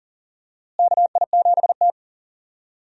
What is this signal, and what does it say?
Morse code, heard as a keyed tone, about 40 wpm, 700 Hz. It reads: KI7T
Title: USA Call Signs - 4 Characters - Repeated 3x - 40wpm